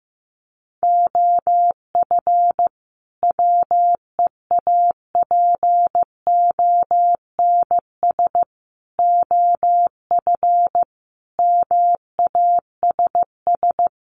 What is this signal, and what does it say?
Morse code, heard as a keyed tone, about 14 wpm, 700 Hz. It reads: OF WEAPONS OF MASS
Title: Sets of 5 Words Encore 15wpm